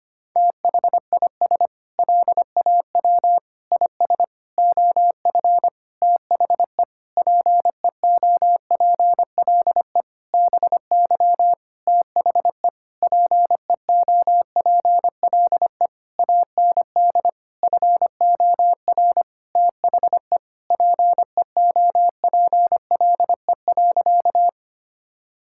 Morse code, heard as a keyed tone, about 25 wpm, 700 Hz. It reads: THIS LAW IS OF THE PEOPLE BY THE PEOPLE AND FOR THE PEOPLE.
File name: practice / 25wpm / Sentences with 9-Word Phrases - 25wpm